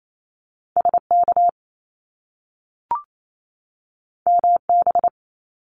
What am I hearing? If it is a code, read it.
Morse code, heard as a keyed tone, about 28 wpm, 700 Hz. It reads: SK MB
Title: Top Two Letter Combinations in Words 28wpm